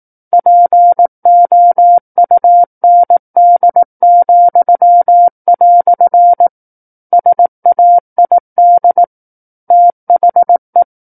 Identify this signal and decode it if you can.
Morse code, heard as a keyed tone, about 18 wpm, 700 Hz. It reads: POUND," SAID THE
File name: 18 WPM